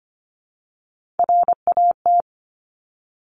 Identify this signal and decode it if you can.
Morse code, heard as a keyed tone, about 25 wpm, 700 Hz. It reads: RAT
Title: Top Three Letter Combinations in Words 25wpm